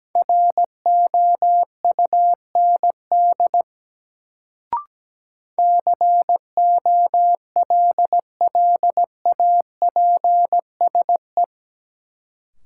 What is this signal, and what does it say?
Morse code, heard as a keyed tone, about 17 wpm, 700 Hz. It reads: ROUND COLLAPSE
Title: Sets of 5 Words Encore - New Word Drill - 17wpm